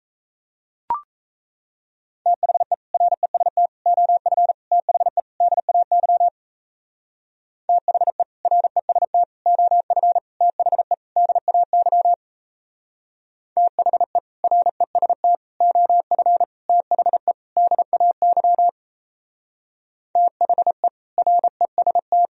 Morse code, heard as a keyed tone, about 34 wpm, 700 Hz. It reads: THE REST OF THE DAY THE REST OF THE DAY THE REST OF THE DAY THE REST
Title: Speed-Racing - Sets of 5 Words 28wpm